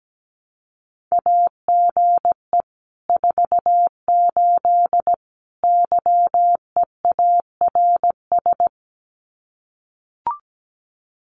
Morse code, heard as a keyed tone, about 17 wpm, 700 Hz. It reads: AGE 48 YEARS
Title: Common QSO Phrases Encore 17wpm